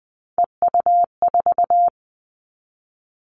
Morse code, heard as a keyed tone, about 20 wpm, 700 Hz. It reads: EU4
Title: Callsign Prefixes for Top 50 DXCC - 20wpm